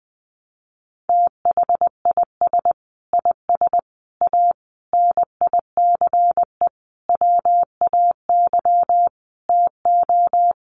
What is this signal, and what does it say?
Morse code, heard as a keyed tone, about 20 wpm, 700 Hz. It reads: THIS IS A NICE WAY TO
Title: Sentences from Top 1100 Words 20wpm